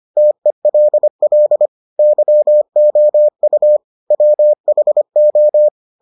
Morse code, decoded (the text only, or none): TELL YOU WHO